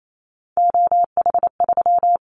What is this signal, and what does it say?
Morse code, heard as a keyed tone, about 28 wpm, 700 Hz. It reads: OH3
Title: Callsign Prefixes for Top 25 DXCC - 28wpm